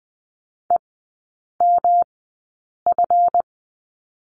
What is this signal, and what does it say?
Morse code, heard as a keyed tone, about 20 wpm, spaced as for 8 wpm, 700 Hz. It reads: EMF
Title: 166 Lesson 27e - Number 7 - Three Characters 20wpm